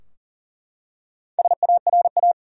Morse code, heard as a keyed tone, about 39 wpm, 700 Hz. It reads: SARA
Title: USA Common Names - Max 4 Letters - 40wpm